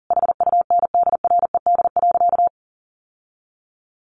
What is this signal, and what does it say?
Morse code, heard as a keyed tone, about 40 wpm, 700 Hz. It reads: HUNDRED.
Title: Sentences from Top 1300 Words 40wpm